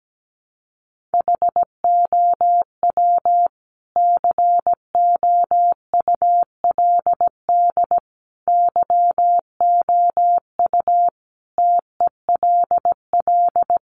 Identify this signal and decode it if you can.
Morse code, heard as a keyed tone, about 17 wpm, 700 Hz. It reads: HOW COULD YOU TELL
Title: Sentences from Top 1000 Words 17wpm